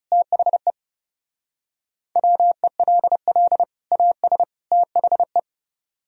Morse code, heard as a keyed tone, about 37 wpm, 700 Hz. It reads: THE WELL AS THE
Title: Speed-Racing - Sets of 3 Words 30wpm